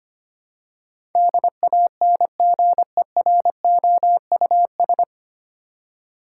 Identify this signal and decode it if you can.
Morse code, heard as a keyed tone, about 25 wpm, 700 Hz. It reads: DANGEROUS
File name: Nine Letter Words - 25wpm